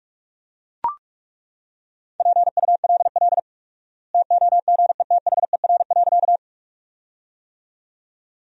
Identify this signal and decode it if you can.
Morse code, heard as a keyed tone, about 42 wpm, 700 Hz. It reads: PULL TOGETHER.
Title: 4x Word-Spacing - Sentences from Top 400 Words - 45wpm